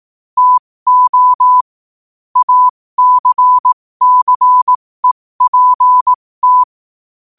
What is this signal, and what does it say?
Morse code, heard as a keyed tone, about 18 wpm, spaced as for 15 wpm, 1000 Hz. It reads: TO ACCEPT
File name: qotdfiles / 15wpm